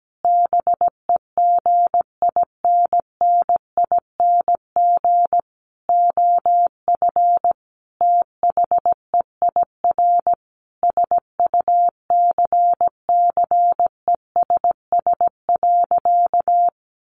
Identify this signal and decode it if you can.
Morse code, heard as a keyed tone, about 17 wpm, 700 Hz. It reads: BEGINNING OF THEIR SUCCESS.
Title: Sentences from Top 900 Words 17wpm